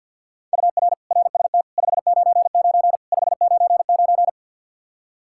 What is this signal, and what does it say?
Morse code, heard as a keyed tone, about 48 wpm, 700 Hz. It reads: UR RST 599 599
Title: Practice QSO - Exchange 1 - Beginning - 50wpm